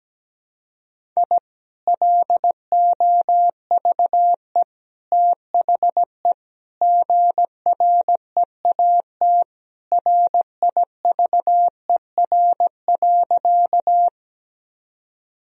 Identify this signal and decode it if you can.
Morse code, heard as a keyed tone, about 17 wpm, 700 Hz. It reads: I LOVE THE GREAT RIVER.
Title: Sentences from Top 1000 Words - Review Part 1 - 17wpm